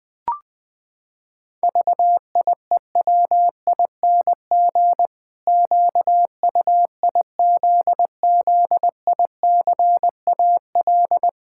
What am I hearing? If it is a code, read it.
Morse code, heard as a keyed tone, about 21 wpm, 700 Hz. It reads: VIEWING QUIZZICAL
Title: Pangram Challenge v2 - 20wpm